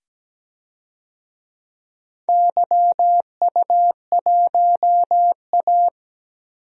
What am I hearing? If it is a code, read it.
Morse code, heard as a keyed tone, about 17 wpm, 700 Hz. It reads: YU1A
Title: DXCC-100 Call Signs - 4 Characters - 17wpm